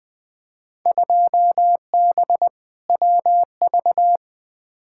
Morse code, heard as a keyed tone, about 20 wpm, 700 Hz. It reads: 2B WV
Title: Field Day Exchanges Encore - 20wpm